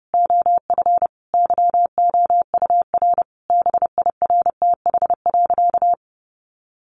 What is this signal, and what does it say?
Morse code, heard as a keyed tone, about 30 wpm, 700 Hz. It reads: OF YOUR BIRTH.
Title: Sentences from Top 1300 Words 30wpm